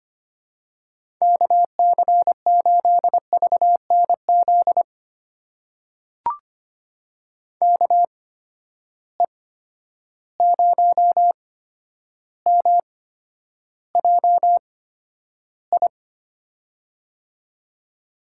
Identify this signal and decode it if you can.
Morse code, heard as a keyed tone, about 25 wpm, spaced as for 11 wpm, 700 Hz. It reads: KC8VNZ KE0MJI
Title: USA Call Signs - 6 Characters - 8x Character Spacing - 25wpm